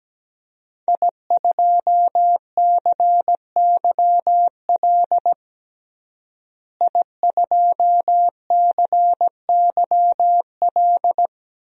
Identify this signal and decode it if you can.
Morse code, heard as a keyed tone, about 17 wpm, 700 Hz. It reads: I2CYL I2CYL
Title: DXCC-100 Call Signs - 5 Characters - Repeated 3x - 17wpm